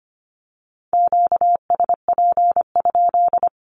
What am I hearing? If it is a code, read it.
Morse code, heard as a keyed tone, about 25 wpm, 700 Hz. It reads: QSP?
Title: Q-Codes with Spoken Meaning - 25wpm